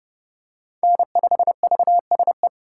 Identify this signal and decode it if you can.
Morse code, heard as a keyed tone, about 30 wpm, 700 Hz. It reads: N5VSE
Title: USA Call Signs - 4x Character Spacing - 30wpm